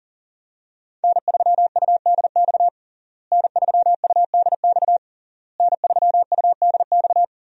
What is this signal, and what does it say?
Morse code, heard as a keyed tone, about 40 wpm, 700 Hz. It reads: N3UDX N3UDX N3UDX
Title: USA Call Signs - 5 Characters - Repeated 3x - 40wpm